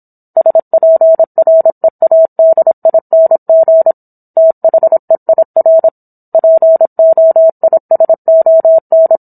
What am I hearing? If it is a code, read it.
Morse code, heard as a keyed tone, about 27 wpm, 650 Hz. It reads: SPREADING THEIR POISON